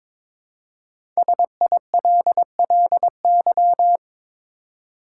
Morse code, heard as a keyed tone, about 22 wpm, 700 Hz. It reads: SILLY
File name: 5 Letter Words - First 3 Letters Given - 22wpm